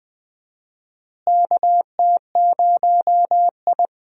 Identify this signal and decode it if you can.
Morse code, heard as a keyed tone, about 21 wpm, 700 Hz. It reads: KT0I
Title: USA Call Signs - 4 Characters - Repeated 2x - 20wpm